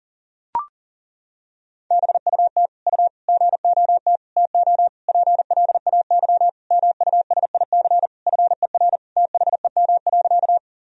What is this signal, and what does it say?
Morse code, heard as a keyed tone, about 40 wpm, 700 Hz. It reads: BUT U GOT TO PLAY MUSIC FER THEM.
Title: Sentences from Top 300 Words and Abbreviations 40wpm